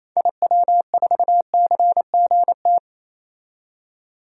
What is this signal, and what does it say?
Morse code, heard as a keyed tone, about 28 wpm, 700 Hz. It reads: IW4CGT
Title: DXCC-100 Call Signs - 6 Characters - 28wpm